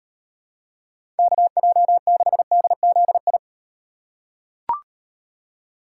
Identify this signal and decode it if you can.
Morse code, heard as a keyed tone, about 38 wpm, 700 Hz. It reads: KJ6DZI